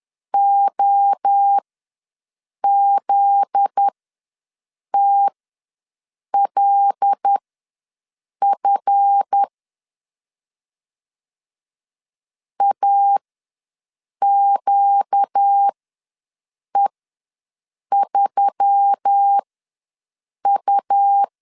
Morse code, heard as a keyed tone, about 10 wpm, spaced as for 6 wpm, 800 Hz. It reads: OZTLF AQE3U
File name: LEZIONI / 7 GF9Z / 7 GF9Z ws 10wpm 3919